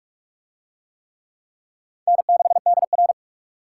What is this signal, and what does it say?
Morse code, heard as a keyed tone, about 40 wpm, 700 Hz. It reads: N6DR